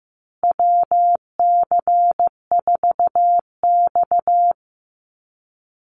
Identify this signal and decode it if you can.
Morse code, heard as a keyed tone, about 15 wpm, 700 Hz. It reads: WC4X
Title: Call Signs Repeated 3x - 15wpm